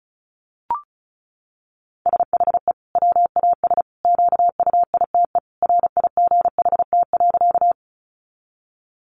Morse code, heard as a keyed tone, about 34 wpm, 700 Hz. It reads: SHE WAS QUITE RIGHT.
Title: Sentences from Top 700 Words 35wpm